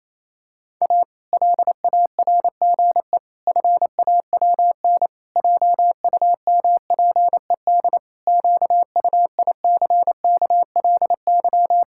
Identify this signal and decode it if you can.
Morse code, heard as a keyed tone, about 28 wpm, 700 Hz. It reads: A LARGE FAWN JUMPED QUICKLY
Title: Pangram Challenge v2 - 28wpm